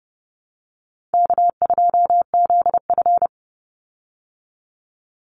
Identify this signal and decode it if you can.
Morse code, heard as a keyed tone, about 30 wpm, 700 Hz. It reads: K2ZF
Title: Easy Call Signs Repeated 3x - 30wpm